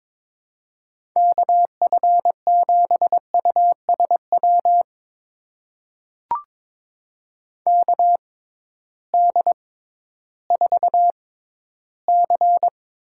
Morse code, heard as a keyed tone, about 22 wpm, spaced as for 13 wpm, 700 Hz. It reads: KF7USW KD4C
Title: USA Call Signs - 6 Characters - 6x Character Spacing - 22wpm